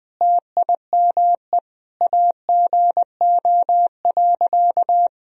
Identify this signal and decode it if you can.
Morse code, heard as a keyed tone, about 20 wpm, 700 Hz. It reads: TIME AGO.
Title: Sentences from Top 1000 Words - Review Part 3 - 20wpm